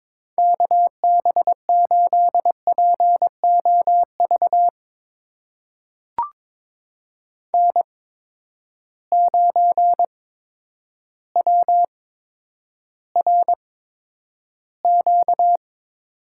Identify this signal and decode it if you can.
Morse code, heard as a keyed tone, about 22 wpm, spaced as for 10 wpm, 700 Hz. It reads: KB8POV N9WRQ
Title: USA Call Signs - 8x Character Spacing - 22wpm